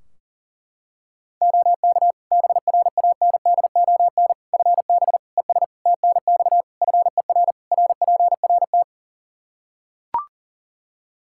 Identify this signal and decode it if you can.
Morse code, heard as a keyed tone, about 40 wpm, 700 Hz. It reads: OK BRANDON FB ES TNX FER RPRT
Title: K1USN SST QSO Practice 40wpm